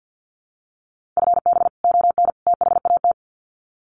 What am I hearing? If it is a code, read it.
Morse code, heard as a keyed tone, about 50 wpm, 700 Hz. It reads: FB ON THAT
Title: Common QSO Phrases 50wpm